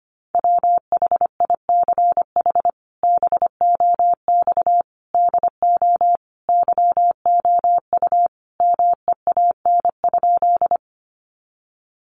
Morse code, heard as a keyed tone, about 25 wpm, 700 Hz. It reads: WHICH BOX DO YOU MEAN?
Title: Sentences from Top 500 Words 25wpm